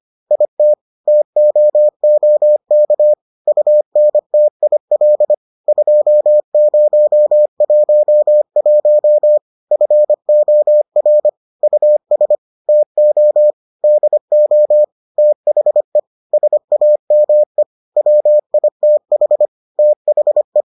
Morse code, heard as a keyed tone, about 25 wpm, 600 Hz. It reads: IT TOOK UNTIL 2011 FOR US TO DO THE SAME WITH THE